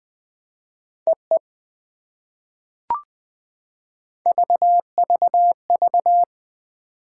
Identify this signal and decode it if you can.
Morse code, heard as a keyed tone, about 20 wpm, 700 Hz. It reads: EE VVV
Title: POTA QSO Practice - 20wpm